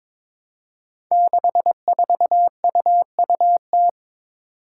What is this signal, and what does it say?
Morse code, heard as a keyed tone, about 21 wpm, 700 Hz. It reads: 64UUT
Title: Five Challenging Letter-Number Combinations 22wpm